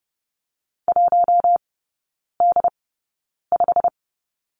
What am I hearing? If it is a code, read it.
Morse code, heard as a keyed tone, about 30 wpm, spaced as for 9 wpm, 700 Hz. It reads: 1D5
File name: Lesson 18e - Letter m - Three Characters 30wpm